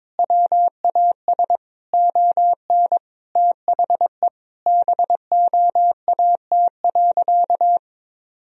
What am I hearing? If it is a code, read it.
Morse code, heard as a keyed tone, about 22 wpm, 700 Hz. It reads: WAS ON THE BOAT.
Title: Sentences from Top 1000 Words - Review Part 3 - 22wpm